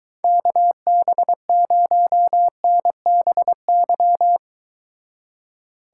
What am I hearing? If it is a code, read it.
Morse code, heard as a keyed tone, about 23 wpm, 700 Hz. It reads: KB0NBY